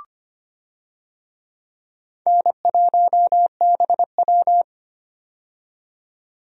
Morse code, heard as a keyed tone, about 25 wpm, 700 Hz. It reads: N1BW
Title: USA Call Signs - 4 Characters - 25wpm